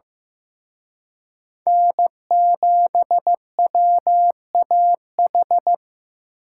Morse code, heard as a keyed tone, about 18 wpm, 700 Hz. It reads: N7WAH
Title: Speed Racing - USA 1x3 Call Signs 15wpm